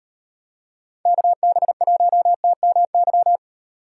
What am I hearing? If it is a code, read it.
Morse code, heard as a keyed tone, about 38 wpm, 700 Hz. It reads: KB1TMY